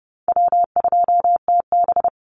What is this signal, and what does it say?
Morse code, heard as a keyed tone, about 30 wpm, 700 Hz. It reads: W2TB
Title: Easy Call Signs Repeated 3x - 30wpm